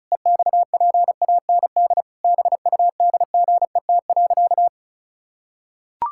Morse code, heard as a keyed tone, about 36 wpm, 700 Hz. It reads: EXPAND BUDGET.
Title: Pangram Challenge v2 - 35wpm